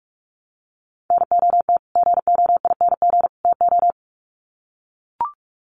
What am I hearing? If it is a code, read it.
Morse code, heard as a keyed tone, about 44 wpm, 700 Hz. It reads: NOT GOING TO
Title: Sets of 3 Words 45wpm